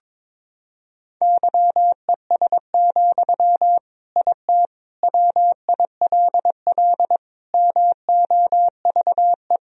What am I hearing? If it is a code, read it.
Morse code, heard as a keyed tone, about 22 wpm, 700 Hz. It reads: YES, IT WILL MOVE
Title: Sentences from Top 1000 Words - Review Part 1 - 22wpm